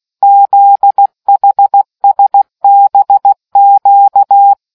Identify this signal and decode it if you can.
Morse code, heard as a keyed tone, about 16 wpm, 800 Hz. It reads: ZHSBQ